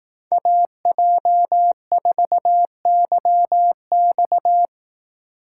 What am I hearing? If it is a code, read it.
Morse code, heard as a keyed tone, about 19 wpm, 700 Hz. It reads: AJ4YX
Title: Speed Racing - USA 2x2 Call Signs 15wpm